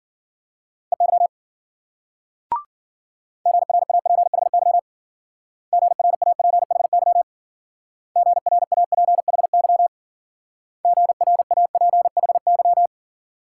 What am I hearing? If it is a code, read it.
Morse code, heard as a keyed tone, about 52 wpm, 700 Hz. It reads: EY GRAPHY GRAPHY GRAPHY GRAPHY